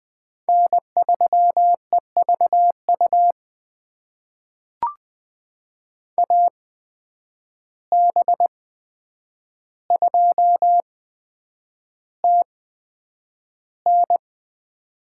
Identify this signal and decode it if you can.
Morse code, heard as a keyed tone, about 20 wpm, spaced as for 9 wpm, 700 Hz. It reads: N3EVU AB2TN
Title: USA Call Signs - 5 Characters - 8x Character Spacing - 20wpm